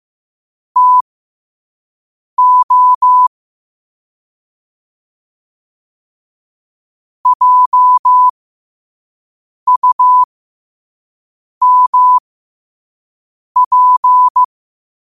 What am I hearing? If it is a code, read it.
Morse code, heard as a keyed tone, about 15 wpm, spaced as for 5 wpm, 1000 Hz. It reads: TO JUMP
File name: qotdfiles / 5wpm